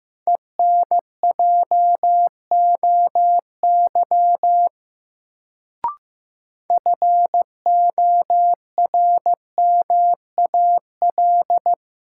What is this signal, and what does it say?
Morse code, heard as a keyed tone, about 15 wpm, 700 Hz. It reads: ENJOY FORMAL
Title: Sentences from Top 1300 Words - New Word Drill Alt - 15wpm